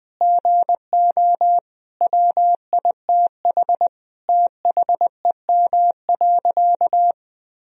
Morse code, heard as a keyed tone, about 21 wpm, 700 Hz. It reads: GO WITH THEM.
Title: Sentences from Top 1000 Words - Review Part 4 - 20wpm